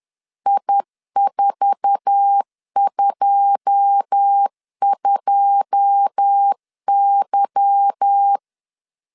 Morse code, heard as a keyed tone, about 10 wpm, 800 Hz. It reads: I422Y
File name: LEZIONI / 9 2 10wpm 2736